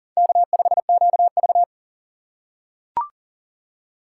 Morse code, heard as a keyed tone, about 40 wpm, 700 Hz. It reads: K5QV
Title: USA Call Signs - 4 Characters - 4x Character Spacing - 40wpm